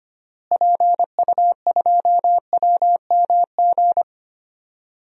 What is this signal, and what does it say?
Morse code, heard as a keyed tone, about 25 wpm, 700 Hz. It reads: PU2WMG
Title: DXCC-100 Call Signs - 6 Characters - Repeated 2x - 25wpm